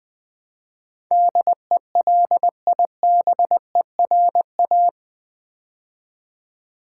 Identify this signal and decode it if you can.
Morse code, heard as a keyed tone, about 20 wpm, 700 Hz. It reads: DELIBERA
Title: Build A Word - 4 Letters At A Time - 12-Letter Words - 20wpm